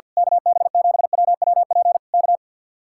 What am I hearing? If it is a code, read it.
Morse code, heard as a keyed tone, about 48 wpm, 700 Hz. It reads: KB7WWP K
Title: Speed-Racing Common QSO Phrases Encore - 50wpm